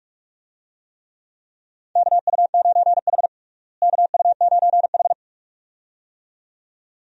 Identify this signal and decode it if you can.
Morse code, heard as a keyed tone, about 46 wpm, 700 Hz. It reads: KU9H KU9H
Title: USA Call Signs - 4 Characters - Repeated 2x - 45wpm